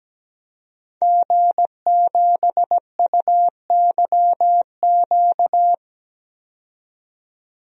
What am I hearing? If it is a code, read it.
Morse code, heard as a keyed tone, about 17 wpm, 700 Hz. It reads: G7UYQ
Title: DXCC-100 Call Signs - 5 Characters - 17wpm